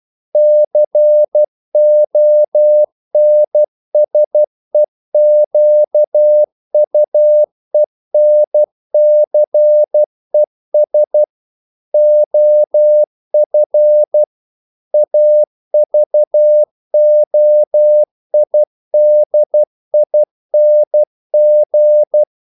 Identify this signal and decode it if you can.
Morse code, heard as a keyed tone, about 12 wpm, 600 Hz. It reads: CONSEQUENCES OF AVOIDING
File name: CW / Practice / EnglishQuotations0112